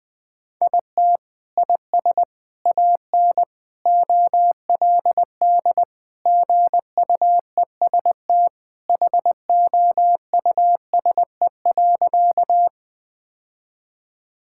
Morse code, heard as a keyed tone, about 20 wpm, 700 Hz. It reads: IT IS AN OLD GUEST HOUSE.
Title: Sentences from Top 1600 Words 20wpm